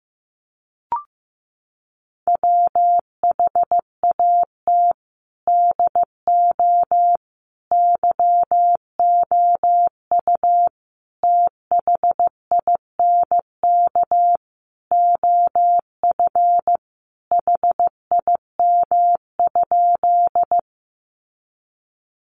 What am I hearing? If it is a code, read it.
Morse code, heard as a keyed tone, about 15 wpm, 700 Hz. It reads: WHAT DO YOU THINK OF HIM?
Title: Sentences from Top 200 Words 15wpm